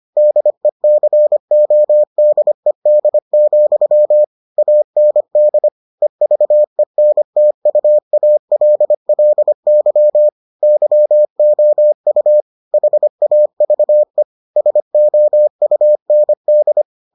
Morse code, heard as a keyed tone, about 25 wpm, 600 Hz. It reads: DECODED, AND EVENTUALLY YOU HAVE SOUND